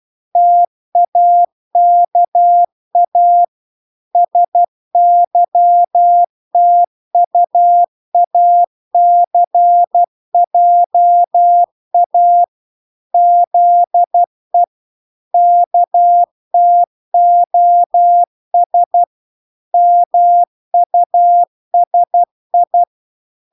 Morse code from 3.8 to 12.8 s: SYTUACJA